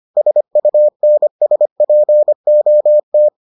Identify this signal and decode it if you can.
Morse code, heard as a keyed tone, about 25 wpm, 600 Hz. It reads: SUNSPOT